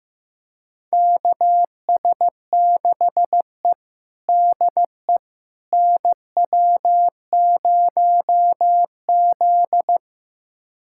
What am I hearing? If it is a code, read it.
Morse code, heard as a keyed tone, about 15 wpm, 700 Hz. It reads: KS6E DE NW0Z
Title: USA Callsign DE Callsign - 4 Characters - 15wpm